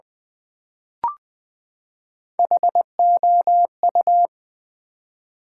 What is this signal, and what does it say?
Morse code, heard as a keyed tone, about 20 wpm, 700 Hz. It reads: HOU
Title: Build A Word - 3 Letters At A Time - 9-Letter Words - 20wpm